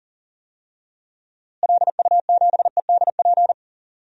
Morse code, heard as a keyed tone, about 40 wpm, 700 Hz. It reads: LU7EDP
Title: DXCC-100 Call Signs - 6 Characters - Repeated 2x - 40wpm